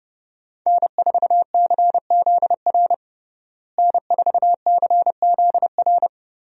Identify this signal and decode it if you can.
Morse code, heard as a keyed tone, about 32 wpm, 700 Hz. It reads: N4CZR N4CZR
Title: USA Call Signs - 5 Characters - Repeated 2x - 30wpm